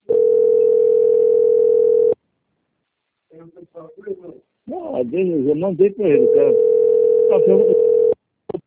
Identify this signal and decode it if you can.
Ringback tone (North America), 440 + 480 Hz, with speech.
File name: out- - - - - .7621